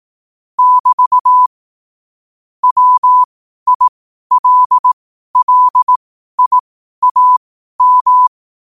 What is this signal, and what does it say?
Morse code, heard as a keyed tone, about 18 wpm, spaced as for 12 wpm, 1000 Hz. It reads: <BT> WILLIAM
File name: qotdfiles / 12wpm